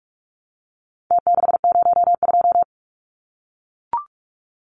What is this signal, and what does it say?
Morse code, heard as a keyed tone, about 44 wpm, 700 Hz. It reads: T602
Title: Numbers - Contest Style - 4-Digit 45wpm